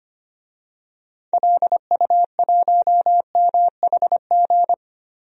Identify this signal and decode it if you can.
Morse code, heard as a keyed tone, about 25 wpm, 700 Hz. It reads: LU1MHG
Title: DXCC-100 Call Signs - 6 Characters - 25wpm